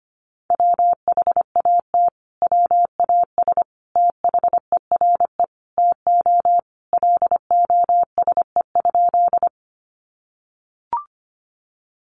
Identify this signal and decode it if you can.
Morse code, heard as a keyed tone, about 25 wpm, 700 Hz. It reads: WHAT WAS THERE TO LOSE?
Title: Sentences from Top 1100 Words 25wpm